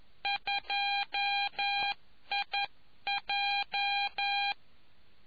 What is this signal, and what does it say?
Morse code, heard as a keyed tone, about 11 wpm, 800 Hz. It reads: 2IJ